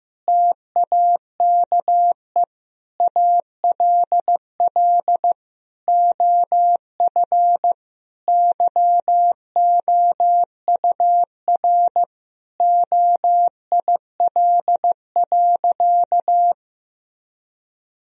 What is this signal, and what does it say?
Morse code, heard as a keyed tone, about 15 wpm, 700 Hz. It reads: TAKE ALL OF YOUR OIL.
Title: Sentences from Top 1000 Words - Review Part 4 - 15wpm